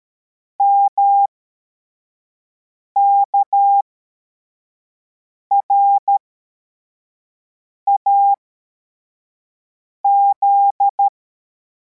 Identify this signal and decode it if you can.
Morse code, heard as a keyed tone, about 13 wpm, spaced as for 4 wpm, 800 Hz. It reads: MKRAZ